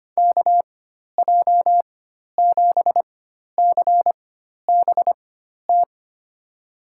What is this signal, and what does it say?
Morse code, heard as a keyed tone, about 25 wpm, spaced as for 12 wpm, 700 Hz. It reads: KJ7CBT
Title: USA Call Signs - 6 Characters - 4x Character Spacing - 25wpm